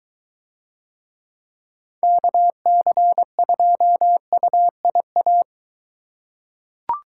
Morse code, heard as a keyed tone, about 23 wpm, 700 Hz. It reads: KC2UIA